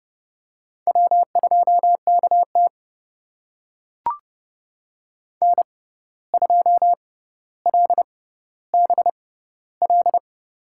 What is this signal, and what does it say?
Morse code, heard as a keyed tone, about 30 wpm, spaced as for 14 wpm, 700 Hz. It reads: W2KT N2LBL
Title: USA Call Signs - 6x Character Spacing - 30wpm